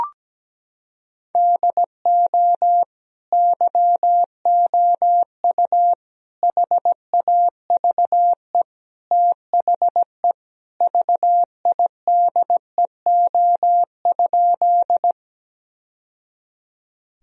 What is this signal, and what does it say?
Morse code, heard as a keyed tone, about 17 wpm, 700 Hz. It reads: DO YOU HAVE THE VIDEO?
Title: Sentences from Top 1000 Words - Review Part 4 - 17wpm